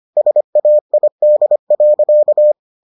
Morse code, heard as a keyed tone, about 25 wpm, 600 Hz. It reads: SAID.